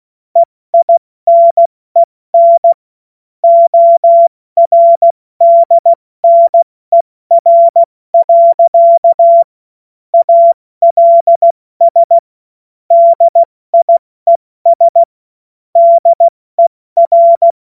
Morse code, heard as a keyed tone, about 16 wpm, spaced as for 14 wpm, 680 Hz. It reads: EINEN ORDNER. ALS DIES DER
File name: Witz-07-Oktober-2025-70-F80